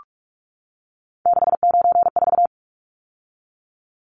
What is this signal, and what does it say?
Morse code, heard as a keyed tone, about 44 wpm, 700 Hz. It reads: 694
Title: Numbers - 3-Digit 45wpm